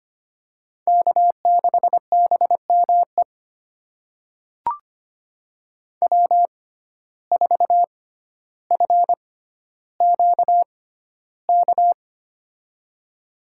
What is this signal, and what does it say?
Morse code, heard as a keyed tone, about 25 wpm, spaced as for 13 wpm, 700 Hz. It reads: K6BME W4FQK
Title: USA Call Signs - 5 Characters - 6x Character Spacing - 25wpm